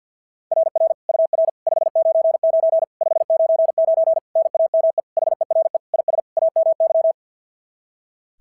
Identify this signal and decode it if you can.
Morse code, heard as a keyed tone, about 53 wpm, 660 Hz. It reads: RR UR 599 599 NAME HERE IS AMY
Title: SOTA QSO Practice 50wpm